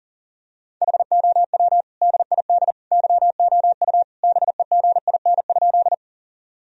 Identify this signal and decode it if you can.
Morse code, heard as a keyed tone, about 40 wpm, 700 Hz. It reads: HOW DID YOU BEGIN?
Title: Sentences from Top 1000 Words - Review Part 1 - 40wpm